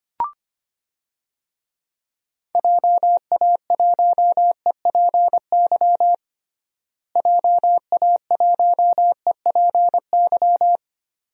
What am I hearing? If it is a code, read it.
Morse code, heard as a keyed tone, about 25 wpm, 700 Hz. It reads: JA1EPY JA1EPY
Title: DXCC-100 Call Signs - 6 Characters - Repeated 2x - 25wpm